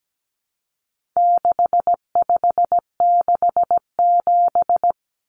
Morse code, heard as a keyed tone, about 17 wpm, 700 Hz. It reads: 6567
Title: Numbers - 4-Digit 17wpm